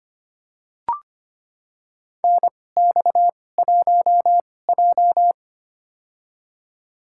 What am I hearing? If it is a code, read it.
Morse code, heard as a keyed tone, about 25 wpm, spaced as for 18 wpm, 700 Hz. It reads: NX1J
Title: USA Call Signs - 4 Characters - 2x Character Spacing - 25wpm